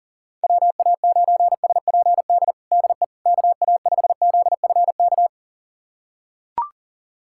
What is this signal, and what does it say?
Morse code, heard as a keyed tone, about 40 wpm, 700 Hz. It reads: WA9SPD DE KA5ZFK
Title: USA Callsign DE Callsign - 6 Characters - 40wpm